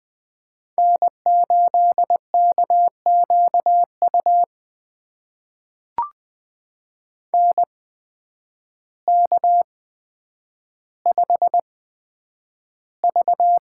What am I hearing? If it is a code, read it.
Morse code, heard as a keyed tone, about 20 wpm, spaced as for 9 wpm, 700 Hz. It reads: N8KQU NK5V
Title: USA Call Signs - 8x Character Spacing - 20wpm